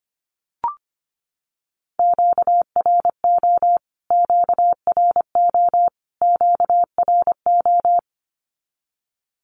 Morse code, heard as a keyed tone, about 25 wpm, 700 Hz. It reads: QRO QRO QRO
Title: Q-Codes Repeated 3x 25wpm